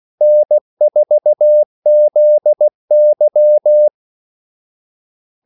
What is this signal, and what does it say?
Morse code, heard as a keyed tone, about 16 wpm, 600 Hz. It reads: N4ZY
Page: Kaldesignaler 16 wpm